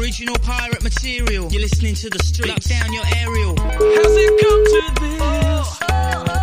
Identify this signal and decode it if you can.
Ringback tone (Europe, 425 Hz), with noise.